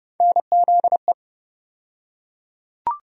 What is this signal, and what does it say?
Morse code, heard as a keyed tone, about 30 wpm, 700 Hz. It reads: NZE
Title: USA Call Sign Postfixes - 3 Letters 30wpm